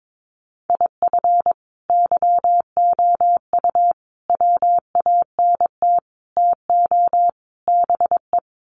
Morse code, heard as a keyed tone, about 22 wpm, 700 Hz. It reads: IF YOU WANT TO BE
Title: Sets of 5 Words Encore 22wpm